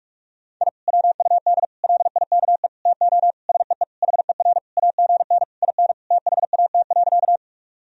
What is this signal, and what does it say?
Morse code, heard as a keyed tone, about 46 wpm, 700 Hz. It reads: I WUD LIKE TO SEE HER AGN IN THAT.
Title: Sentences from Top 200 Words and Abbreviations 45wpm